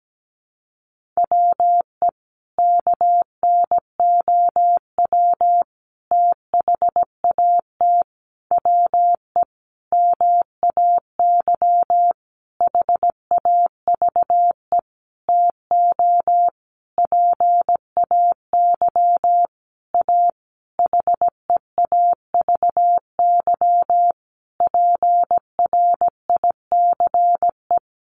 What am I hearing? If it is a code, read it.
Morse code, heard as a keyed tone, about 17 wpm, 700 Hz. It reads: WE KNOW THAT WE MAY HAVE TO PAY A HEAVY PRICE
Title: Sentences from Top 1100 Words 17wpm